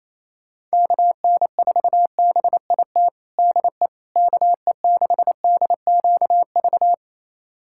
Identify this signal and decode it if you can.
Morse code, heard as a keyed tone, about 27 wpm, 700 Hz. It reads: KN4BIT DE KE6DQV
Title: USA Callsign DE Callsign - 6 Characters - 28wpm